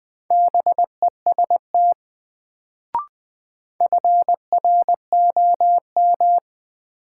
Morse code, heard as a keyed tone, about 20 wpm, 700 Hz. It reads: BEST FROM
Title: Sets of 2 Words - New Word Drill - 20wpm